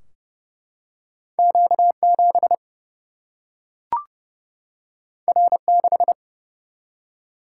Morse code, heard as a keyed tone, about 30 wpm, 700 Hz. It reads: Q 7 R 6
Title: Single-Letter Single-Number 30wpm